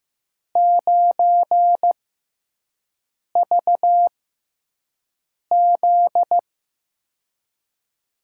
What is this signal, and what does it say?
Morse code, heard as a keyed tone, about 15 wpm, spaced as for 5 wpm, 700 Hz. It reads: 9VZ
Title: USA Call Signs - 5 Characters - 6x Character Spacing - 15wpm